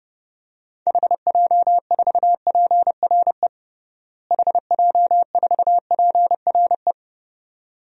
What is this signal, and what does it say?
Morse code, heard as a keyed tone, about 32 wpm, 700 Hz. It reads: HJ4PRE HJ4PRE
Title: DXCC-100 Call Signs - 6 Characters - Repeated 2x - 30wpm